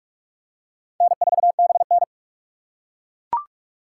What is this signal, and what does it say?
Morse code, heard as a keyed tone, about 42 wpm, 700 Hz. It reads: N4BN
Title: USA Call Signs - 6x Character Spacing - 45wpm